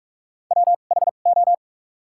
Morse code, heard as a keyed tone, about 43 wpm, spaced as for 32 wpm, 700 Hz. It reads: WHO
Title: 2x Character Spacing - Top 100 Words - 45wpm